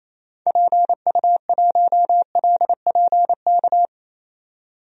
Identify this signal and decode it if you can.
Morse code, heard as a keyed tone, about 28 wpm, 700 Hz. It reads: PU1LPK
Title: DXCC-100 Call Signs - 6 Characters - 28wpm